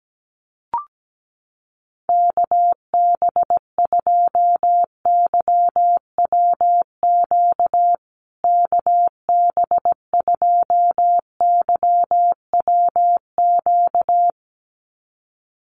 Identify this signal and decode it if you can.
Morse code, heard as a keyed tone, about 17 wpm, 700 Hz. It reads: KB2YWQ KB2YWQ
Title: USA Call Signs Repeated 2x 17wpm